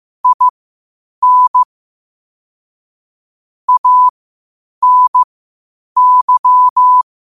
Morse code, heard as a keyed tone, about 15 wpm, spaced as for 8 wpm, 1000 Hz. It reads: IN ANY
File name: qotdfiles / 8wpm